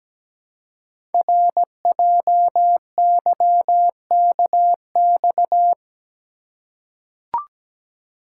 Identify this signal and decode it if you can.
Morse code, heard as a keyed tone, about 17 wpm, 700 Hz. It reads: RJYKX
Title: Code Groups - 5 Letters - 17wpm